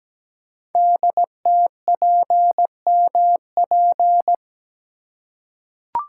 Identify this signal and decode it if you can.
Morse code, heard as a keyed tone, about 17 wpm, 700 Hz. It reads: DTPMP
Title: Code Groups - 5 Letters - 17wpm